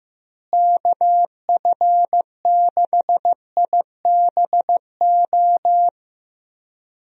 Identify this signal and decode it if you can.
Morse code, heard as a keyed tone, about 15 wpm, 700 Hz. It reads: KF6IBO
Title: USA Call Signs - 6 Characters - Repeated 2x - 15wpm